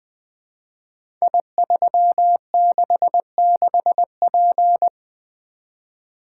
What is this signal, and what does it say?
Morse code, heard as a keyed tone, about 21 wpm, 700 Hz. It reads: I366P
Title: Code Groups - 5 Letter:Numbers - 20wpm